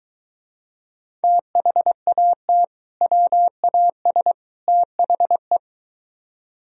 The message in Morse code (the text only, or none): THAT WAS THE